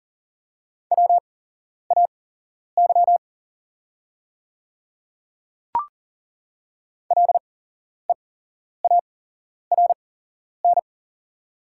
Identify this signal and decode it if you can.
Morse code, heard as a keyed tone, about 40 wpm, spaced as for 11 wpm, 690 Hz. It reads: WAY LEARN
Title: 8x Character Spacing - Top 300 Words - 40wpm